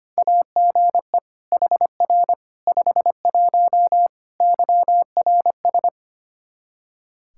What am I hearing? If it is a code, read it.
Morse code, heard as a keyed tone, about 25 wpm, 700 Hz. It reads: AGE HR 51 YRS
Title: Practice QSO - Exchange 3 - More Info - 25wpm